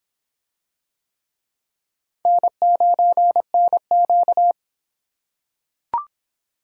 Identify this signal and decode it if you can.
Morse code, heard as a keyed tone, about 23 wpm, 700 Hz. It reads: N9NQ